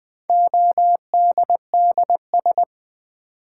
Morse code, heard as a keyed tone, about 20 wpm, 700 Hz. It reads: ODDS
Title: Build A Word - 1 Letter At A Time - 3 to 5-Letter Words - 20wpm